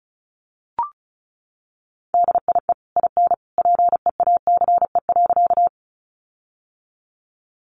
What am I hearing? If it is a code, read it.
Morse code, heard as a keyed tone, about 34 wpm, 700 Hz. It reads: DIE IN PEACE.
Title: Sentences from Top 1500 Words 35wpm